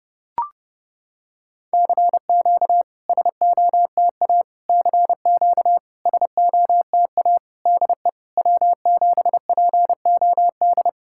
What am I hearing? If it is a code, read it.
Morse code, heard as a keyed tone, about 32 wpm, 700 Hz. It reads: CQ SOTA CQ SOTA DE W7POD
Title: SOTA QSO Practice 30wpm